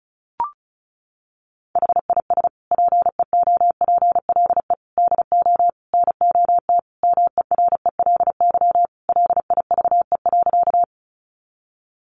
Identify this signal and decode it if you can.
Morse code, heard as a keyed tone, about 34 wpm, 700 Hz. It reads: HIS PEOPLE DO NOT MERELY LIVE.
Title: Sentences from Top 1400 Words 35wpm